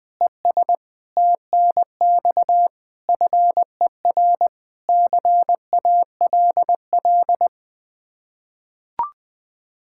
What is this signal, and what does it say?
Morse code, heard as a keyed tone, about 21 wpm, 700 Hz. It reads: ES TNX FER CALL
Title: K1USN SST QSO Practice 20wpm